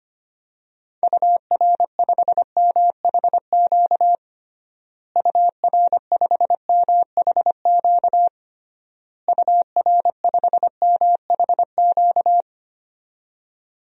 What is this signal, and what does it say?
Morse code, heard as a keyed tone, about 25 wpm, 700 Hz. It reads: UR5MHQ UR5MHQ UR5MHQ
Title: DXCC-100 Call Signs - 6 Characters - Repeated 3x - 25wpm